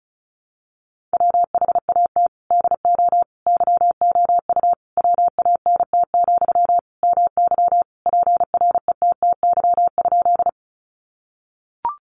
Morse code, heard as a keyed tone, about 34 wpm, 700 Hz. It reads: WHAT DO YOU WANT, MY PRETTY?
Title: Sentences from Top 1000 Words 35wpm